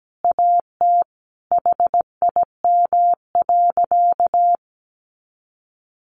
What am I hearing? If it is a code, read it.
Morse code, heard as a keyed tone, about 17 wpm, 700 Hz. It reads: AT HIM.
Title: Additional Easy Phrases 17wpm